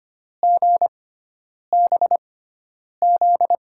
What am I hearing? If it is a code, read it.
Morse code, heard as a keyed tone, about 25 wpm, spaced as for 9 wpm, 700 Hz. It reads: GBZ
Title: USA Call Signs - 6 Characters - 6x Character Spacing - 25wpm